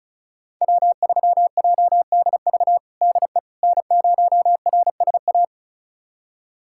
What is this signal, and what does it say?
Morse code, heard as a keyed tone, about 34 wpm, 700 Hz. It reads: W3JDV DE N0RSA
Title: USA Callsign DE Callsign - 5 Characters - 35wpm